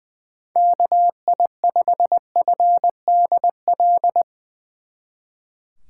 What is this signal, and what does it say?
Morse code, heard as a keyed tone, about 20 wpm, 700 Hz. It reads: KI5FDL
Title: Speed Racing - USA 2x3 Call Signs 20wpm